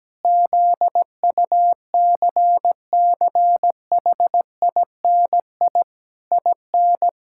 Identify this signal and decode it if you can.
Morse code, heard as a keyed tone, about 17 wpm, 700 Hz. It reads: ZUCCHINI IN
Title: Pangram Challenge v2 - 17wpm